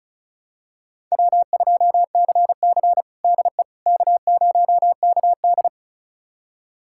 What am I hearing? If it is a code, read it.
Morse code, heard as a keyed tone, about 34 wpm, 700 Hz. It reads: W2CC DE K0KD
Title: USA Callsign DE Callsign - 4 Characters - 35wpm